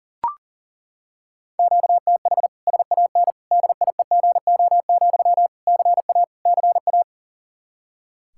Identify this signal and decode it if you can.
Morse code, heard as a keyed tone, about 37 wpm, 700 Hz. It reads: QTH SAN DIEGO, CA CA
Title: Practice QSO - Exchange 1 - Beginning - 40wpm